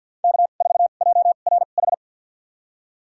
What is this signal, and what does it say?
Morse code, heard as a keyed tone, about 51 wpm, spaced as for 36 wpm, 700 Hz. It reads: K4JRH
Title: USA Call Signs - 5 Characters - 2x Character Spacing - 50wpm